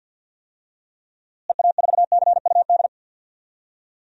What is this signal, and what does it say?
Morse code, heard as a keyed tone, about 52 wpm, 700 Hz. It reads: EA4CUD
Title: DXCC-100 Call Signs - 6 Characters - Repeated 2x - 50wpm